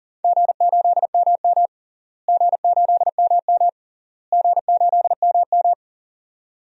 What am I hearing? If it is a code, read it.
Morse code, heard as a keyed tone, about 40 wpm, 700 Hz. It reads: G8MM G8MM G8MM
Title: DXCC-100 Call Signs - 4 Characters - Repeated 3x - 40wpm